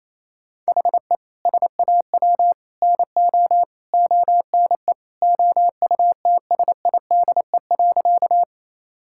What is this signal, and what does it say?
Morse code, heard as a keyed tone, about 28 wpm, 700 Hz. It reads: HE SAW NO ONE OUTSIDE.
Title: Sentences from Top 1000 Words - Review Part 6 - 28wpm